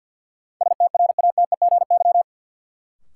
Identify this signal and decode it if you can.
Morse code, heard as a keyed tone, about 52 wpm, 700 Hz. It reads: STRATEGY
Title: Sentences from Top 1000 Words - Review Part 5 - New Word Drill - 50wpm